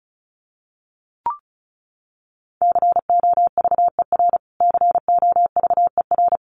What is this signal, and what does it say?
Morse code, heard as a keyed tone, about 34 wpm, 700 Hz. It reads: COVER COVER
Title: Top 500 Words Repeated 35wpm